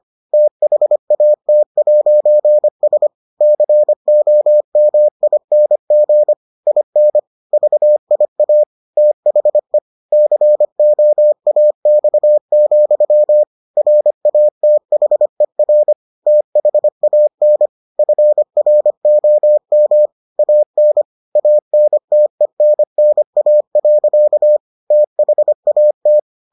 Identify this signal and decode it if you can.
Morse code, heard as a keyed tone, about 25 wpm, 600 Hz. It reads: THAT'S COMING IN VIA THE COAX, RATHER THAN FROM AN ANTENNA. THAT